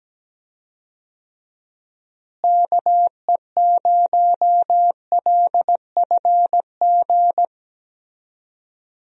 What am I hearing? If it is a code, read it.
Morse code, heard as a keyed tone, about 17 wpm, 700 Hz. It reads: KE0LFG
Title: USA Call Signs - 6 Characters - Repeated 2x - 17wpm